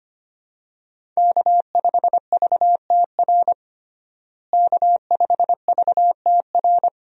Morse code, heard as a keyed tone, about 25 wpm, 700 Hz. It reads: K5VTR K5VTR
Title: USA Call Signs - 5 Characters - Repeated 2x - 25wpm